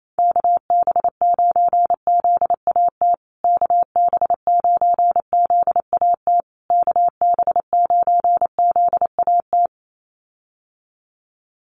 Morse code, heard as a keyed tone, about 28 wpm, 700 Hz. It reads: KB9ZAT KB9ZAT KB9ZAT
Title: USA Call Signs Repeated 3x 28wpm